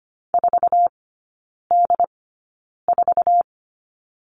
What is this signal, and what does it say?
Morse code, heard as a keyed tone, about 25 wpm, spaced as for 9 wpm, 700 Hz. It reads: 4D4
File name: Lesson 14e - Number 2 - Three Characters 25wpm